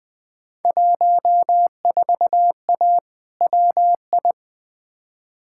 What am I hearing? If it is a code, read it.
Morse code, heard as a keyed tone, about 20 wpm, 700 Hz. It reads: 14A WI
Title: Field Day Exchanges Encore - 20wpm